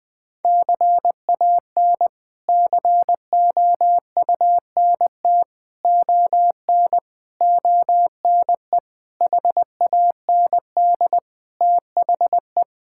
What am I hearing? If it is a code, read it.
Morse code, heard as a keyed tone, about 20 wpm, 700 Hz. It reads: CAN COUNT ON ONE HAND THE
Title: Sets of 8 Words - 20wpm